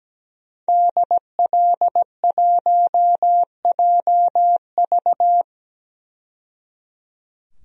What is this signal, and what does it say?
Morse code, heard as a keyed tone, about 17 wpm, 700 Hz. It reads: DL1JV
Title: DXCC-100 Call Signs - 5 Characters - 17wpm